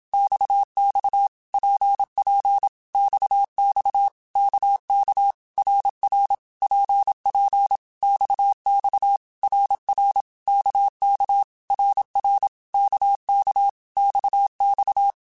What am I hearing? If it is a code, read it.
Morse code, heard as a keyed tone, about 27 wpm, 800 Hz. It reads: XX PP XX KK RR PP XX RR KK RR KK XX
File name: Basic Letter Groups By7Groups / RKPX / RKPX@25